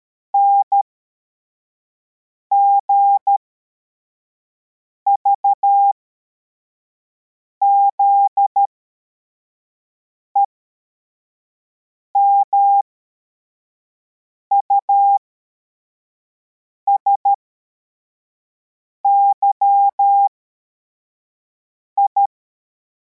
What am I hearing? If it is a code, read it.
Morse code, heard as a keyed tone, about 13 wpm, spaced as for 4 wpm, 800 Hz. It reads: NGVZE MUSYI
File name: formation / cw / Text55